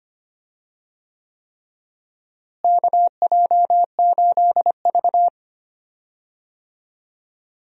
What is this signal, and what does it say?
Morse code, heard as a keyed tone, about 25 wpm, 700 Hz. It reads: KJ8V
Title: USA Call Signs - 4 Characters - Repeated 3x - 25wpm